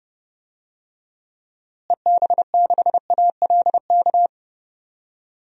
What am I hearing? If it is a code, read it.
Morse code, heard as a keyed tone, about 30 wpm, 700 Hz. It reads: EB6ALK
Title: DXCC-100 Call Signs - 6 Characters - Repeated 3x - 30wpm